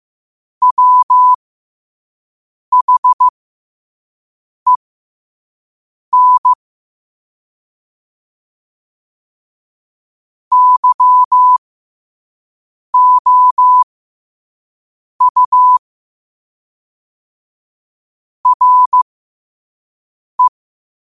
Morse code, heard as a keyed tone, about 15 wpm, spaced as for 5 wpm, 1000 Hz. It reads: WHEN YOU RE
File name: qotdfiles / 5wpm